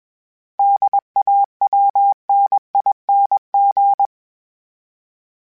Morse code, heard as a keyed tone, about 21 wpm, 800 Hz. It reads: DAWNING
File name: Special INGs / Special INGs-mp3s / Sngl Ing @20